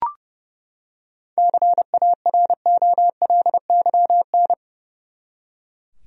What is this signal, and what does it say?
Morse code, heard as a keyed tone, about 30 wpm, 700 Hz. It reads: CAROLYN
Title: USA Common Names - 30wpm